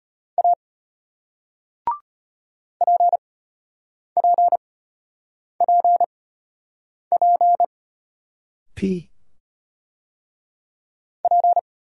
Morse code, heard as a keyed tone, about 34 wpm, spaced as for 8 wpm, 700 Hz. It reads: A PPPP P